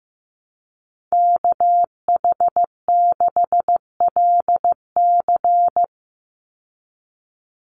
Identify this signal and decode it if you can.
Morse code, heard as a keyed tone, about 15 wpm, 700 Hz. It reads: KH6LC
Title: Call Signs Repeated 3x - 15wpm